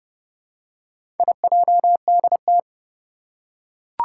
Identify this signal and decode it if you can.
Morse code, heard as a keyed tone, about 30 wpm, 700 Hz. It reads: IJDT
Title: Four Challenging Letter-Number Combinations 30wpm